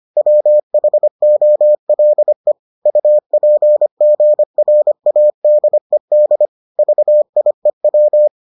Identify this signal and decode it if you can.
Morse code, heard as a keyed tone, about 25 wpm, 600 Hz. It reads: WHOLE UPGRADED VIEW